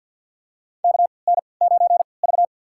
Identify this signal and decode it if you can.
Morse code, heard as a keyed tone, about 53 wpm, spaced as for 29 wpm, 700 Hz. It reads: KN9V
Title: USA Call Signs - 4 Characters - 3x Character Spacing - 50wpm